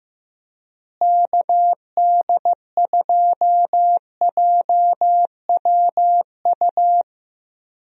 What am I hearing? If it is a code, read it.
Morse code, heard as a keyed tone, about 15 wpm, 700 Hz. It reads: KD2JWU
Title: USA Call Signs - 6 Characters - Repeated 3x - 15wpm